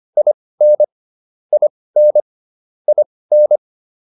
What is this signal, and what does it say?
Morse code, heard as a keyed tone, about 25 wpm, spaced as for 18 wpm, 600 Hz. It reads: IN IN IN